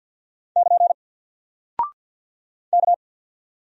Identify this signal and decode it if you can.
Morse code, heard as a keyed tone, about 50 wpm, 700 Hz. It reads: <KN> K
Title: Single Character (Letter:Number:Punctuation:Prosign) 50wpm